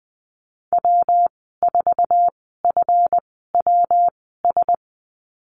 Morse code, heard as a keyed tone, about 20 wpm, spaced as for 14 wpm, 700 Hz. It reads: W4FWS
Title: 147 Lesson 24g - Letter y - Call Signs 20wpm